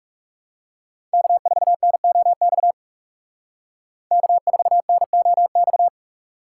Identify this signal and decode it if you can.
Morse code, heard as a keyed tone, about 38 wpm, 700 Hz. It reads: K4NOX K4NOX